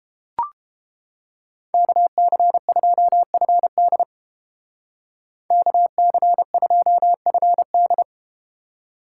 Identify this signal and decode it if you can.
Morse code, heard as a keyed tone, about 31 wpm, 700 Hz. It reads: KC2FD KC2FD